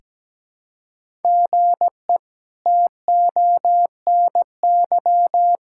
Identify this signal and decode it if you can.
Morse code, heard as a keyed tone, about 17 wpm, 700 Hz. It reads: GE TONY
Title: K1USN SST QSO Practice - 17wpm